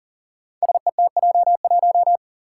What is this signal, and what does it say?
Morse code, heard as a keyed tone, about 40 wpm, 700 Hz. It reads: SETJ1
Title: Five Challenging Letter-Number Combinations 40wpm